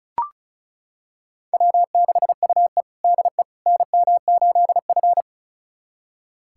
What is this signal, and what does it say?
Morse code, heard as a keyed tone, about 36 wpm, 700 Hz. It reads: W6UE DE NM8F
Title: USA Callsign DE Callsign - 4 Characters - 35wpm